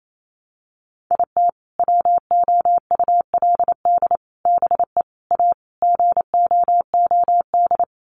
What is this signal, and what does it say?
Morse code, heard as a keyed tone, about 28 wpm, 700 Hz. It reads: IT WOULD BE A GOOD
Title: Sets of 5 Words Encore2 28wpm